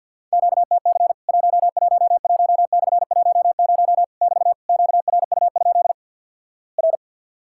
Morse code, heard as a keyed tone, about 50 wpm, 700 Hz. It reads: QTC 111/10 <BT> QRU? R
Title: Worked All Europe DX Contest - QTC Exchange with Repeats - 50wpm